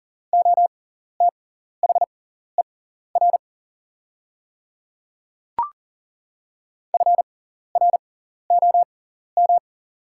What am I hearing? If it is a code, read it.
Morse code, heard as a keyed tone, about 40 wpm, spaced as for 14 wpm, 700 Hz. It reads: OTHER FROM
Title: 6x Character Spacing - Top 200 Words - 40wpm